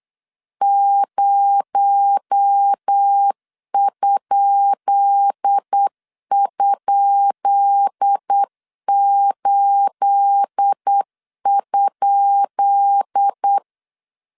Morse code, heard as a keyed tone, about 8 wpm, 800 Hz. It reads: 0??8?
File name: LEZIONI / 10 SIGNS / 10B INT 8wpm 3534